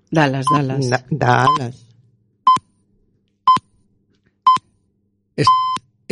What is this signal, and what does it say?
Radio time signal (time pips), with speech. It is heard from 0.5 to 5.7 s.